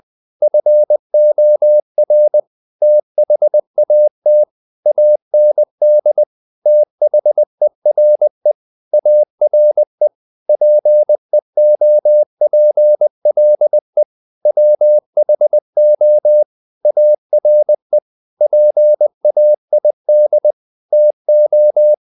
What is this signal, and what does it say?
Morse code, heard as a keyed tone, about 20 wpm, 600 Hz. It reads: FOR THAT AND THERE ARE PEOPLE WHO ARE PAID TO
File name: the old man and the sea part 2 20wpm